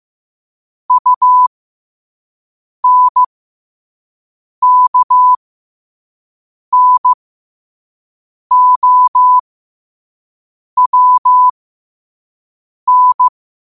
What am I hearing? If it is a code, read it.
Morse code, heard as a keyed tone, about 15 wpm, spaced as for 5 wpm, 1000 Hz. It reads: UNKNOWN
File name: qotdfiles / 5wpm